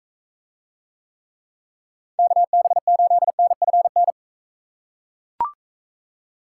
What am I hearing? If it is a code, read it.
Morse code, heard as a keyed tone, about 42 wpm, 700 Hz. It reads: KB8NFN